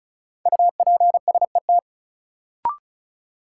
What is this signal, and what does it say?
Morse code, heard as a keyed tone, about 36 wpm, 700 Hz. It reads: UPSET
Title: Sentences from Top 800 Words - New Word Drill - 35wpm